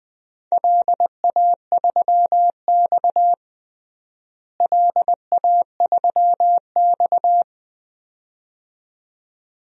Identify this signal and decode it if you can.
Morse code, heard as a keyed tone, about 21 wpm, 700 Hz. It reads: LA3X LA3X
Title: DXCC-100 Call Signs - 4 Characters - Repeated 2x - 20wpm